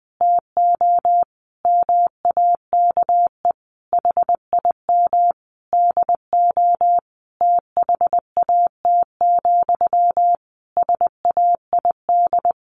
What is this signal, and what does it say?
Morse code, heard as a keyed tone, about 20 wpm, 700 Hz. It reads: TO MAKE HIM DO THAT, SAID
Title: A Sound Of Thunder 20wpm